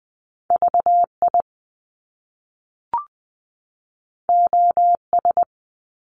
Morse code, heard as a keyed tone, about 20 wpm, 700 Hz. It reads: V I O S
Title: Top Two Letter Combinations in Words 20wpm